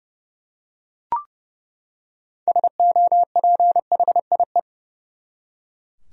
Morse code, heard as a keyed tone, about 30 wpm, 700 Hz. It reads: SOPHIE
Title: USA Common Names - 30wpm